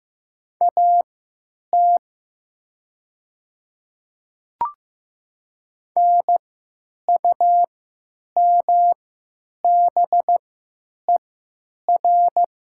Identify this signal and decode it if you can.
Morse code, heard as a keyed tone, about 15 wpm, spaced as for 9 wpm, 700 Hz. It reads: AT NUMBER
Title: 3x Character Spacing - Top 100 Words - 15wpm